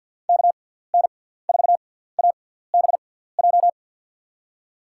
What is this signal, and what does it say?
Morse code, heard as a keyed tone, about 47 wpm, spaced as for 17 wpm, 700 Hz. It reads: KN4ABJ
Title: USA Call Signs - 6 Characters - 6x Character Spacing - 50wpm